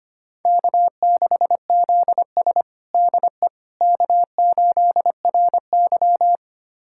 Morse code, heard as a keyed tone, about 25 wpm, 700 Hz. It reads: K6ZS DE K8RY
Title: USA Callsign DE Callsign - 4 Characters - 25wpm